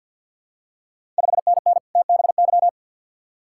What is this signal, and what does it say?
Morse code, heard as a keyed tone, about 52 wpm, 700 Hz. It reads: 5NN TBY
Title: Ontario QSO Party Exchange 50wpm